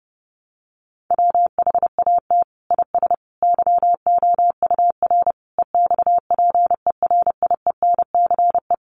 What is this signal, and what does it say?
Morse code, heard as a keyed tone, about 30 wpm, 700 Hz. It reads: WHAT IS YOUR EXPERIENCE
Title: Sentences from Top 600 Words 30wpm